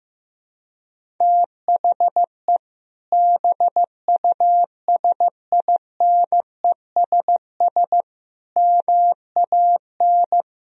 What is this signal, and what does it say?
Morse code, heard as a keyed tone, about 15 wpm, 700 Hz. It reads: THE BUSINESS MAN
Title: Sentences from Top 1000 Words - Review Part 1 - 15wpm